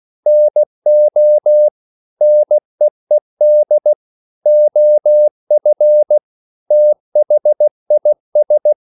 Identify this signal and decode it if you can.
Morse code, heard as a keyed tone, about 16 wpm, 600 Hz. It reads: NO NEED OF THIS
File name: CW / Practice / EnglishQuotations0416